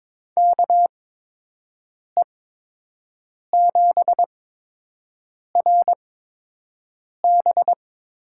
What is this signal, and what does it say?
Morse code, heard as a keyed tone, about 22 wpm, spaced as for 6 wpm, 700 Hz. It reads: KE7RB
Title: USA Call Signs - 6 Characters - 8x Character Spacing - 22wpm